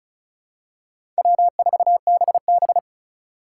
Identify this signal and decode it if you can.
Morse code, heard as a keyed tone, about 34 wpm, 700 Hz. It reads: W4BB
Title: USA Call Signs - 4 Characters - 3x Character Spacing - 35wpm